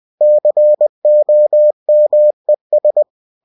Morse code, heard as a keyed tone, about 20 wpm, 600 Hz. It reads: COMES
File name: cw-books / winnie-the-pooh / winnie-20wpm-page-0091